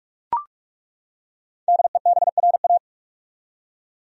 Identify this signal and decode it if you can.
Morse code, heard as a keyed tone, about 43 wpm, 700 Hz. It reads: DEBRA
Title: USA Common Names - 45wpm